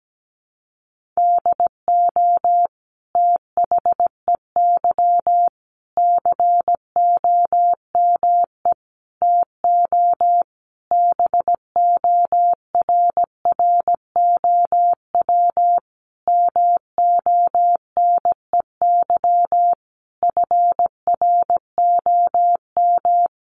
Morse code, heard as a keyed tone, about 17 wpm, 700 Hz. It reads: DO THEY COME TO BORROW MONEY FROM
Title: Sentences from Top 1500 Words 17wpm